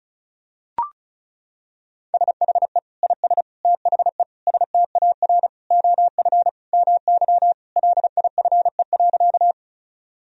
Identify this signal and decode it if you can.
Morse code, heard as a keyed tone, about 36 wpm, 700 Hz. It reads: SHE IS THE STAR OF MY LIFE.
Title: Sentences from Top 1000 Words - Review Part 2 - 35wpm